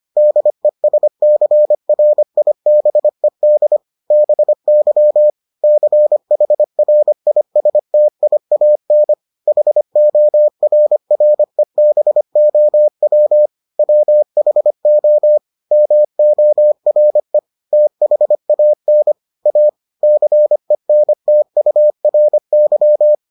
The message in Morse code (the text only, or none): DESCRIBED BY CHRISTIAN HORREBOW WHO MORE THAN A CENTURY